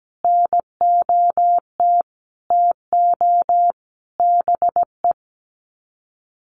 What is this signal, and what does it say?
Morse code, heard as a keyed tone, about 17 wpm, 700 Hz. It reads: NOT TO BE
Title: Sets of 5 Words Encore2 17wpm